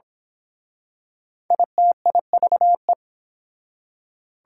Morse code, heard as a keyed tone, about 28 wpm, 690 Hz. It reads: ITIVE